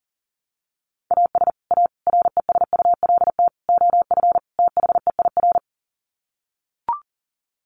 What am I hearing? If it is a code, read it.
Morse code, heard as a keyed tone, about 40 wpm, 700 Hz. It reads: AS A RESULT OF THEIR
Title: Sets of 5 Words Encore2 40wpm